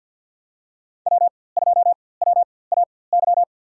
Morse code, heard as a keyed tone, about 52 wpm, spaced as for 23 wpm, 700 Hz. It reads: W2WAY
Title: USA Call Signs - 5 Characters - 4x Character Spacing - 50wpm